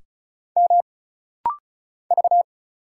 Morse code, heard as a keyed tone, about 36 wpm, 700 Hz. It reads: M V
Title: Single Letter-Number Alt - Rapid-Fire 35wpm